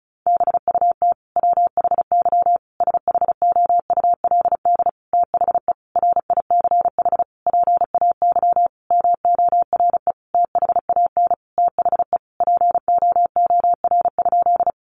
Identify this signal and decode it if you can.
Morse code, heard as a keyed tone, about 34 wpm, 700 Hz. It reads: BUT WHY SHOULD THE RICH PAY MORE THAN THE POOR?
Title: Sentences from Top 1100 Words 35wpm